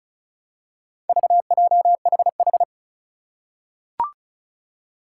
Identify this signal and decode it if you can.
Morse code, heard as a keyed tone, about 36 wpm, 700 Hz. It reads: VJHH
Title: Four Challenging Letter-Number Combinations 35wpm